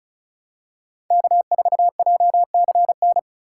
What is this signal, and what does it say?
Morse code, heard as a keyed tone, about 36 wpm, 700 Hz. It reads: K4JCN
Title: USA Call Signs - 5 Characters - 2x Character Spacing - 35wpm